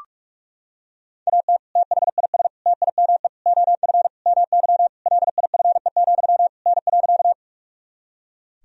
Morse code, heard as a keyed tone, about 46 wpm, 700 Hz. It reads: AT THIS TIME OF MY LIFE, N.
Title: Sentences from Top 300 Words and Abbreviations 45wpm